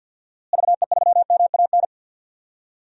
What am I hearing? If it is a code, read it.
Morse code, heard as a keyed tone, about 46 wpm, 700 Hz. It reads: VE3MAN
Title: DXCC-100 Call Signs - 6 Characters - Repeated 2x - 50wpm